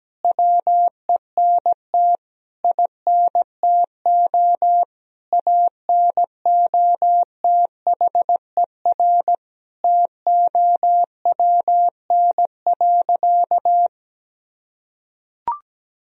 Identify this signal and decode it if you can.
Morse code, heard as a keyed tone, about 17 wpm, 700 Hz. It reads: WENT INTO ANOTHER TOWN.
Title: Sentences from Top 1000 Words - Review Part 2 - 17wpm